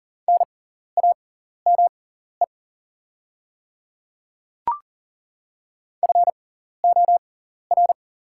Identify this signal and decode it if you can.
Morse code, heard as a keyed tone, about 40 wpm, spaced as for 14 wpm, 690 Hz. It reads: NAME FOR
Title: 6x Character Spacing - Top 300 Words - 40wpm